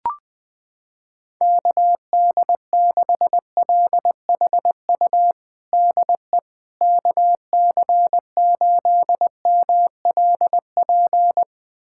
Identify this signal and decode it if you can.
Morse code, heard as a keyed tone, about 20 wpm, 700 Hz. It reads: KD6LHU DE KC8MLP
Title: USA Callsign DE Callsign 20wpm